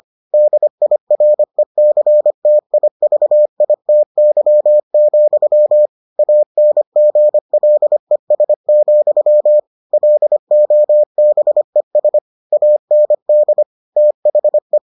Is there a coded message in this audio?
Yes: Morse code from 0.3 to 14.8 s, heard as a keyed tone, about 25 wpm, 600 Hz. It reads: DIRECTIVITY, ANGLES, LOBES AND THE